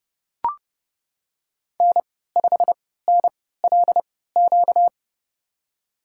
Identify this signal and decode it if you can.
Morse code, heard as a keyed tone, about 29 wpm, spaced as for 17 wpm, 700 Hz. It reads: N5NLQ
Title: USA Call Signs - 5 Characters - 3x Character Spacing - 30wpm